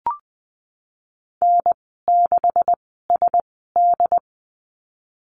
Morse code, heard as a keyed tone, about 20 wpm, spaced as for 15 wpm, 700 Hz. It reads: N6SD
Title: 167 Lesson 27g - Number 7 - Call Signs 20wpm